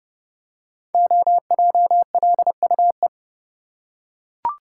Morse code, heard as a keyed tone, about 32 wpm, 700 Hz. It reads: OJLUE
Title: Code Groups - 5 Letter:Numbers - 30wpm